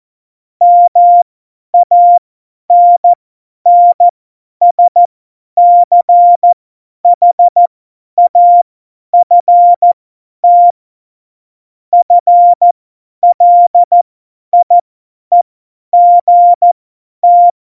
Morse code, heard as a keyed tone, about 14 wpm, spaced as for 10 wpm, 700 Hz. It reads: MANNSCHAFT FLIEGT
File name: Morsekurs / Witze / Witz 22. Dezember 2020, 50 F70